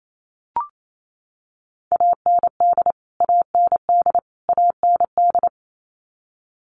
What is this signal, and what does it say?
Morse code, heard as a keyed tone, about 28 wpm, 700 Hz. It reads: AND AND AND
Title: Top 300 Words Repeated 28wpm